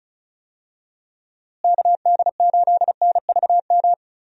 Morse code, heard as a keyed tone, about 36 wpm, 700 Hz. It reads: KD8NVM
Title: USA Call Signs - 6 Characters - Repeated 2x - 35wpm